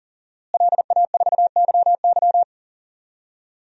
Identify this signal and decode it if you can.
Morse code, heard as a keyed tone, about 39 wpm, 700 Hz. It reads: LA4YY
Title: DXCC-100 Call Signs - 5 Characters - Repeated 2x - 40wpm